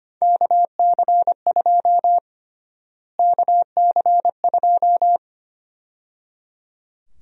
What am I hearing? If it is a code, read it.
Morse code, heard as a keyed tone, about 25 wpm, 700 Hz. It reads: KC2 KC2
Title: USA Call Sign Prefixes - Repeated 2x - 25wpm